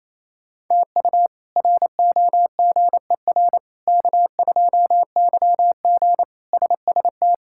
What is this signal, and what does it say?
Morse code, heard as a keyed tone, about 28 wpm, 700 Hz. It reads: TU ROGER K2YG SST
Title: K1USN SST QSO Practice - 28wpm